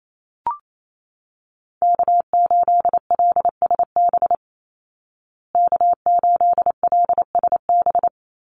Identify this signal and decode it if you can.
Morse code, heard as a keyed tone, about 28 wpm, 700 Hz. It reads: K8LSB K8LSB
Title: Call Signs Repeated 3x - 28wpm